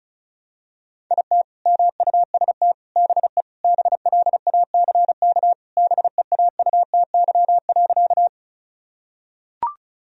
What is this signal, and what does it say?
Morse code, heard as a keyed tone, about 36 wpm, 700 Hz. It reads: IT MUST BE BLACK BEAUTY.
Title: Sentences from Top 1000 Words - Review Part 2 - 35wpm